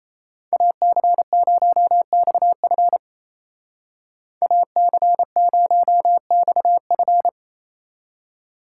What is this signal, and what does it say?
Morse code, heard as a keyed tone, about 33 wpm, 700 Hz. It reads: AC0XF AC0XF
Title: Speed Racing - USA 2x2 Call Signs 28wpm